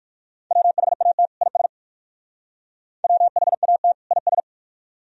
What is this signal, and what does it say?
Morse code, heard as a keyed tone, about 56 wpm, 700 Hz. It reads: WHAT IS WHAT IS